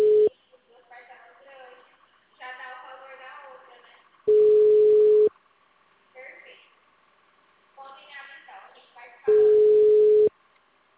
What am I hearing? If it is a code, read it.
Ringback tone (Europe, 425 Hz), with speech.